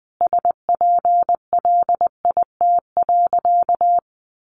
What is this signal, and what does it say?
Morse code, heard as a keyed tone, about 19 wpm, 700 Hz. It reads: SPLIT.
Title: Additional Easy Phrases 20wpm